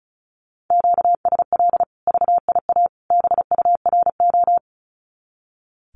Morse code, heard as a keyed tone, about 34 wpm, 700 Hz. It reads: QSL VIA BURO
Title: Common QSO Phrases 35wpm